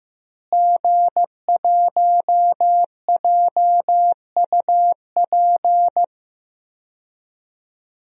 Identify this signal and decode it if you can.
Morse code, heard as a keyed tone, about 15 wpm, 700 Hz. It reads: G1JUP
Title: DXCC-100 Call Signs - 5 Characters - Repeated 3x - 15wpm